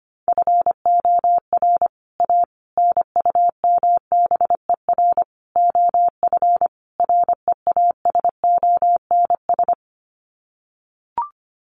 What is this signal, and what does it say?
Morse code, heard as a keyed tone, about 25 wpm, 700 Hz. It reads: FOR A NUMBER OF REASONS
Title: Sets of 5 Words Encore2 25wpm